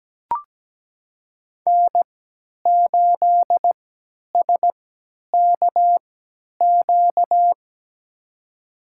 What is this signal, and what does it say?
Morse code, heard as a keyed tone, about 17 wpm, spaced as for 10 wpm, 700 Hz. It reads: N8SKQ
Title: USA Call Signs - 5 Characters - 3x Character Spacing - 17wpm